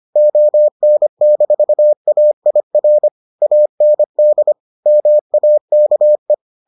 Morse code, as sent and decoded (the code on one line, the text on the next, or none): --- -. -....- .- .. .-. / .- -. -.. / -- .- -.- .
ON-AIR AND MAKE